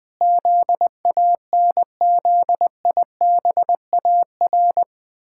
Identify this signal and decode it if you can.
Morse code, heard as a keyed tone, about 20 wpm, 700 Hz. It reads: ZANZIBAR
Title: Pangram Challenge v2 - 20wpm